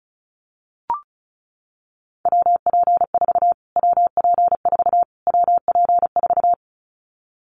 Morse code, heard as a keyed tone, about 34 wpm, 700 Hz. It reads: WP4 WP4 WP4
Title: Callsign Prefixes for Top 25 DXCC - Repeated 3x - 35wpm